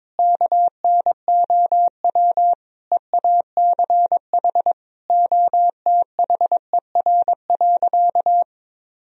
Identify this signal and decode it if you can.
Morse code, heard as a keyed tone, about 22 wpm, 700 Hz. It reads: KNOW EACH OTHER.
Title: Sentences from Top 1000 Words - Review Part 1 - 22wpm